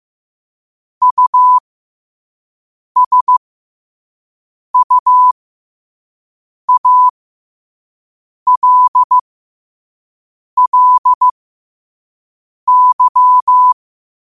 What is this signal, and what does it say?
Morse code, heard as a keyed tone, about 15 wpm, spaced as for 5 wpm, 1000 Hz. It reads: USUALLY
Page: learn-to-code